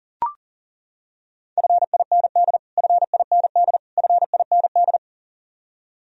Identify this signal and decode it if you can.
Morse code, heard as a keyed tone, about 40 wpm, 700 Hz. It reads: FIND FIND FIND
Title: Top 100 Words Repeated 40wpm